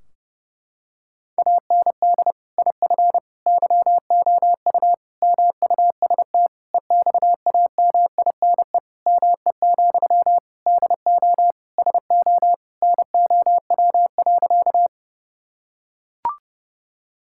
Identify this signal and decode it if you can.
Morse code, heard as a keyed tone, about 30 wpm, 700 Hz. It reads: AND IF YOU MUST EXAMINE ME, DO SO NOW.
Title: Sentences from Top 1600 Words 30wpm